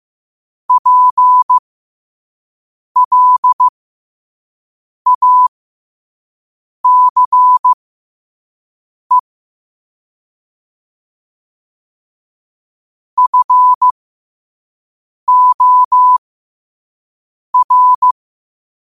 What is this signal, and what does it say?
Morse code, heard as a keyed tone, about 15 wpm, spaced as for 5 wpm, 1000 Hz. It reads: PLACE FOR
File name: qotdfiles / 5wpm